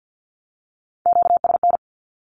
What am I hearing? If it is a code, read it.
Morse code, heard as a keyed tone, about 50 wpm, 700 Hz. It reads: QSN
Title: Q-Codes with Spoken Meaning - 50wpm